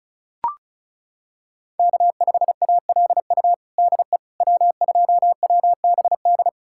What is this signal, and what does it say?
Morse code, heard as a keyed tone, about 34 wpm, 700 Hz. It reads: K5ALU DE W2WBD
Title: USA Callsign DE Callsign - 5 Characters - 35wpm